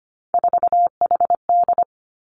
Morse code, heard as a keyed tone, about 25 wpm, 700 Hz. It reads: 4HD
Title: Three Challenging Letter-Number Combinations 25wpm